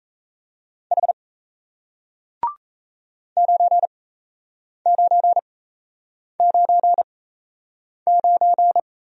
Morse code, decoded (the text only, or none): H 9999